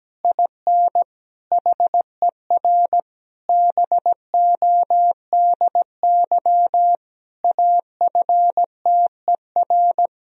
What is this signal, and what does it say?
Morse code, heard as a keyed tone, about 17 wpm, 700 Hz. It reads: IN HER BODY AFTER
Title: Sentences from Top 1000 Words - Review Part 2 - 17wpm